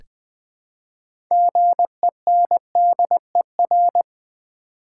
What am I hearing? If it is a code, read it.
Morse code, heard as a keyed tone, about 20 wpm, 700 Hz. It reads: GENDER
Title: Build A Word - 3 Letters At A Time - 6-Letter Words - 20wpm